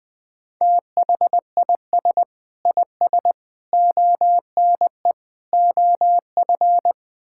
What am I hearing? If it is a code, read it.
Morse code, heard as a keyed tone, about 20 wpm, 700 Hz. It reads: THIS IS ONE OF
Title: Build A Phrase - 4-Word Phrases - 20wpm